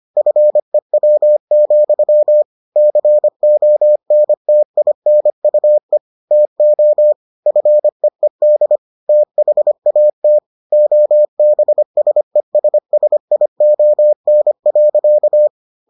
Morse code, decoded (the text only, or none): FEW, CONTINUE TO FEED THAT OBSESSION.